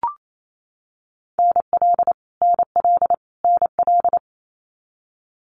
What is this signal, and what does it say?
Morse code, heard as a keyed tone, about 28 wpm, 700 Hz. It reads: NL NL NL
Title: Canadian Provinces and Territories - Repeated 3x - 28wpm